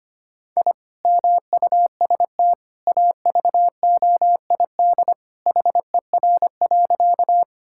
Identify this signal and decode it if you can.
Morse code, heard as a keyed tone, about 25 wpm, 700 Hz. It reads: I MUST AVOID HER.
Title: Sentences from Top 1000 Words - Review Part 6 - 25wpm